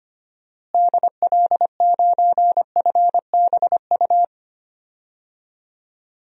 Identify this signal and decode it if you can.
Morse code, heard as a keyed tone, about 25 wpm, 690 Hz. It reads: DL9FBU
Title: DXCC-100 Call Signs - 6 Characters - Repeated 3x - 25wpm